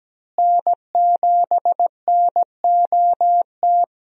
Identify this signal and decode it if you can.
Morse code, heard as a keyed tone, about 17 wpm, 700 Hz. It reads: N7NOT
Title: USA Call Signs - 5 Characters - Repeated 3x - 17wpm